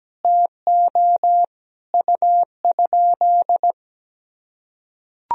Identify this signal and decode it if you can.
Morse code, heard as a keyed tone, about 17 wpm, 700 Hz. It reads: TO U?
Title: Sentences from Top 100 Words and Abbreviations 17wpm